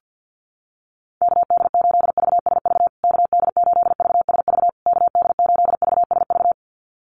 Morse code, heard as a keyed tone, about 50 wpm, 700 Hz. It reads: KD8VSV KD8VSV KD8VSV
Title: USA Call Signs Repeated 3x 50wpm